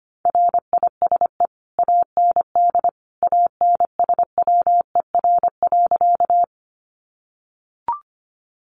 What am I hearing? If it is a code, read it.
Morse code, heard as a keyed tone, about 25 wpm, 700 Hz. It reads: RISE AND ANSWER.
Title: Sentences from Top 900 Words 25wpm